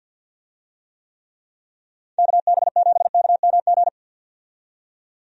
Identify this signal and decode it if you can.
Morse code, heard as a keyed tone, about 51 wpm, 700 Hz. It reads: KB7KMG
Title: USA Call Signs - 6 Characters - 50wpm